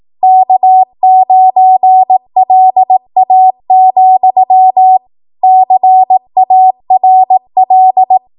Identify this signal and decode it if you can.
Morse code, heard as a keyed tone, about 18 wpm, 750 Hz. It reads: K9LA, CARL
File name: Archive / 18 WPM / 18WPM